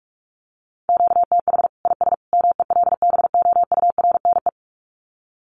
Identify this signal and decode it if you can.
Morse code, heard as a keyed tone, about 44 wpm, 700 Hz. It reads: QTH IS MELBOURNE
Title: Common QSO Phrases 45wpm